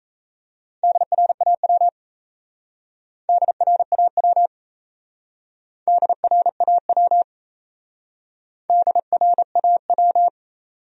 Morse code, heard as a keyed tone, about 37 wpm, 700 Hz. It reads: DRAW DRAW DRAW DRAW